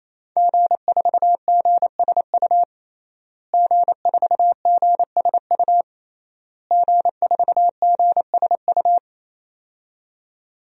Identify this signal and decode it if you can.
Morse code, heard as a keyed tone, about 28 wpm, 700 Hz. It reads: G4GSU G4GSU G4GSU
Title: DXCC-100 Call Signs - 5 Characters - Repeated 3x - 28wpm